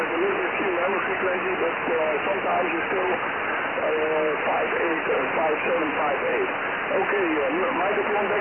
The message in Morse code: none